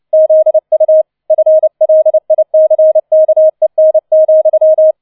Morse code, heard as a keyed tone, about 29 wpm, 630 Hz. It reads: ZU FLICKEN,